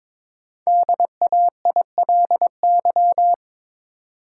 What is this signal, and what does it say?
Morse code, heard as a keyed tone, about 22 wpm, 700 Hz. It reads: DAILY
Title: Sentences from Top 1100 Words - New Word Drill - 22wpm